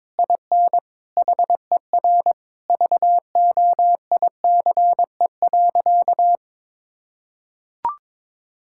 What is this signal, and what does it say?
Morse code, heard as a keyed tone, about 22 wpm, 700 Hz. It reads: IN HER VOICE.
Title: Sentences from Top 1600 Words 22wpm